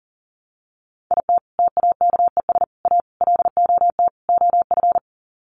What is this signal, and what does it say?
Morse code, heard as a keyed tone, about 37 wpm, 700 Hz. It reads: IT TAKES A LOT OF
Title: Sets of 5 Words Encore 40wpm